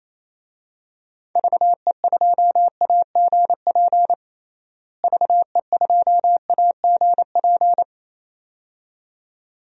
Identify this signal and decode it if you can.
Morse code, heard as a keyed tone, about 28 wpm, 700 Hz. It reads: VE2AGP VE2AGP
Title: DXCC-100 Call Signs - 6 Characters - Repeated 2x - 28wpm